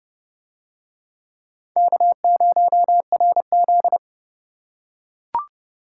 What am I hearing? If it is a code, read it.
Morse code, heard as a keyed tone, about 28 wpm, 700 Hz. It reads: K0RZ